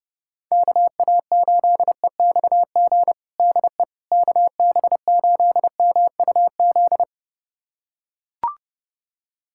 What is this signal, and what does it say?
Morse code, heard as a keyed tone, about 30 wpm, 700 Hz. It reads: KA8EXG DE KB8MUZ
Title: USA Callsign DE Callsign 30wpm